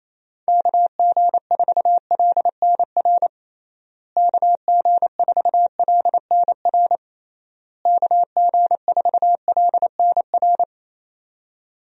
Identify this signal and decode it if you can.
Morse code, heard as a keyed tone, about 28 wpm, 700 Hz. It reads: KG4LNR KG4LNR KG4LNR
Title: USA Call Signs - 6 Characters - Repeated 3x - 28wpm